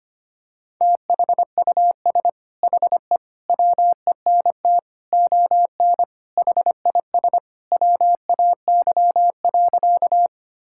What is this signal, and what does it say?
Morse code, heard as a keyed tone, about 25 wpm, 700 Hz. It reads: THUS HE WENT ON HIS WAY.
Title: Sentences from Top 1000 Words - Review Part 5 - 25wpm